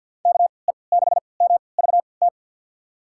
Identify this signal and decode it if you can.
Morse code, heard as a keyed tone, about 51 wpm, spaced as for 28 wpm, 700 Hz. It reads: KE6MVT
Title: USA Call Signs - 6 Characters - 3x Character Spacing - 50wpm